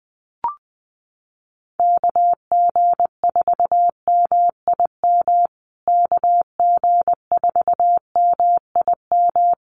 Morse code, heard as a keyed tone, about 20 wpm, 700 Hz. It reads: KG4MIM KG4MIM
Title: USA Call Signs Repeated 3x 20wpm